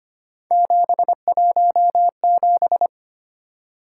Morse code, heard as a keyed tone, about 25 wpm, 700 Hz. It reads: 717
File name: Three Letter-Number Combinations - 25wpm